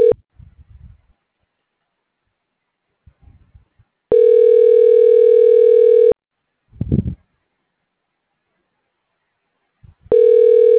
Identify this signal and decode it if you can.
Ringback tone (North America), 440 + 480 Hz.
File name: recordings / RE26f49fa6db7b6408fc697dfd62d5d7f6